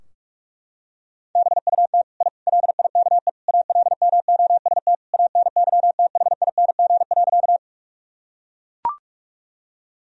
Morse code, heard as a keyed tone, about 41 wpm, 700 Hz. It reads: BUT I LIKE ALMOST ANYTHING.
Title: Sentences from Top 1000 Words - Review Part 4 - 45wpm